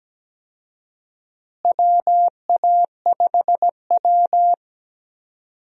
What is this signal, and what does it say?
Morse code, heard as a keyed tone, about 17 wpm, 700 Hz. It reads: WA5W
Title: USA Call Signs - 4 Characters - Repeated 2x - 17wpm